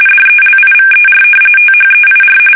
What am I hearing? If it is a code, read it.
RTTY, about 45.45 baud. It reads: DELIVER THEIR
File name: fsk850 / battleshipnj